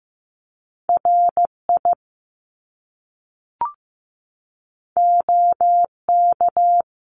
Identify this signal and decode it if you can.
Morse code, heard as a keyed tone, about 15 wpm, 700 Hz. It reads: RI OK
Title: Top Two Letter Combinations in Words 15wpm